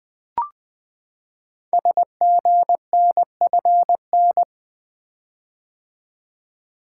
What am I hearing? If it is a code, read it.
Morse code, heard as a keyed tone, about 20 wpm, 700 Hz. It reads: SGNFN
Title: Code Groups - 5 Letters - 20wpm